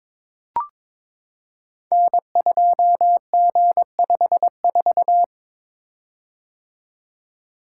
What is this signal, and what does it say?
Morse code, heard as a keyed tone, about 22 wpm, 700 Hz. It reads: N2G54
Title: Code Groups - 5 Letter:Numbers - 22wpm